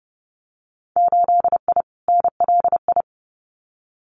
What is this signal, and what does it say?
Morse code, heard as a keyed tone, about 30 wpm, 700 Hz. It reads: 8I NLI
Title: Winter Field Day Exchanges Encore 30wpm